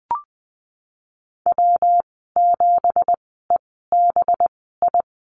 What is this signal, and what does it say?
Morse code, heard as a keyed tone, about 20 wpm, spaced as for 15 wpm, 700 Hz. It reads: W7EBI
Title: 219 Lesson 35g - Number 0 - Call Signs 20wpm